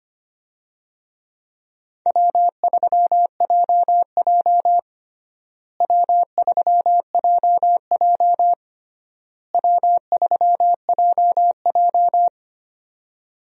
Morse code, heard as a keyed tone, about 25 wpm, 700 Hz. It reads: W3JJ W3JJ W3JJ
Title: USA Call Signs - 4 Characters - Repeated 3x - 25wpm